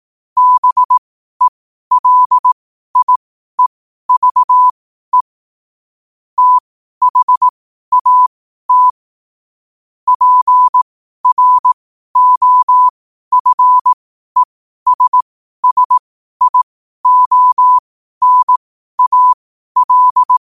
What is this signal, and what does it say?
Morse code, heard as a keyed tone, about 18 wpm, spaced as for 12 wpm, 1000 Hz. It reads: BELIEVE THAT PROFESSIONAL
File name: 12wpm